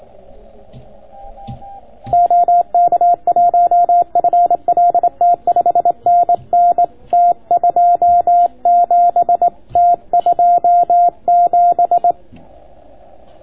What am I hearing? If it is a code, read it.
Morse code, heard as a keyed tone, about 24 wpm, 700 Hz. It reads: OK1FLT 5NN T27 T27